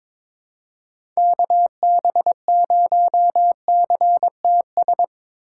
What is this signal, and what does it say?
Morse code, heard as a keyed tone, about 22 wpm, 700 Hz. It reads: KB0CTS
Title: USA Call Signs - 6 Characters - 3x Character Spacing - 22wpm